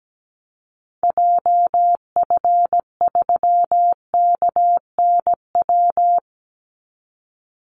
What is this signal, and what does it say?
Morse code, heard as a keyed tone, about 17 wpm, 700 Hz. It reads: JF3KNW
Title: Call Signs Repeated 3x - 17wpm